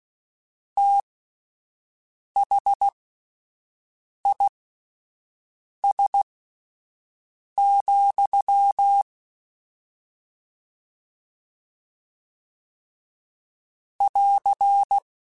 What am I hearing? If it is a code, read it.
Morse code, heard as a keyed tone, about 16 wpm, spaced as for 5 wpm, 800 Hz. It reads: THIS, <AR>